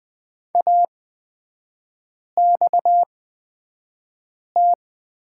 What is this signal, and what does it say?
Morse code, heard as a keyed tone, about 20 wpm, spaced as for 5 wpm, 700 Hz. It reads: AXT
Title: 250 Lesson 40e - BK - Three Characters 20wpm